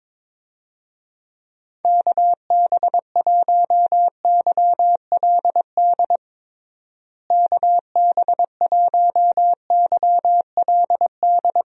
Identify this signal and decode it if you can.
Morse code, heard as a keyed tone, about 22 wpm, 700 Hz. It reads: KB1YLD KB1YLD
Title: USA Call Signs - 6 Characters - Repeated 2x - 22wpm